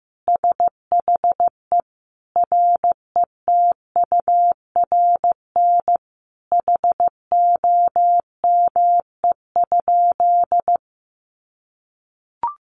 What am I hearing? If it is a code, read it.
Morse code, heard as a keyed tone, about 14 wpm, 700 Hz. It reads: SHE RETURN HOME?
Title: Sentences from Top 800 Words 15wpm